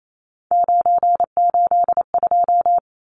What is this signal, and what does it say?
Morse code, heard as a keyed tone, about 28 wpm, 700 Hz. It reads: 982
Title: Numbers - 3-Digit 28wpm